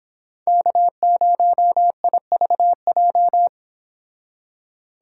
Morse code, heard as a keyed tone, about 26 wpm, 700 Hz. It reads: K0IVJ
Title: Speed Racing - USA 1x3 Call Signs 22wpm